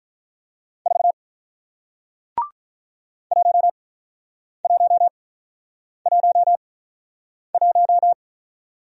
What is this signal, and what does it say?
Morse code, heard as a keyed tone, about 46 wpm, spaced as for 9 wpm, 700 Hz. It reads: 4 1111